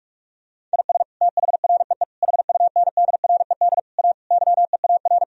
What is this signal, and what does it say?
Morse code, heard as a keyed tone, about 43 wpm, 700 Hz. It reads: IS THREE HUNDRED A YEAR
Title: Sentences from Top 1000 Words - Review Part 2 - 45wpm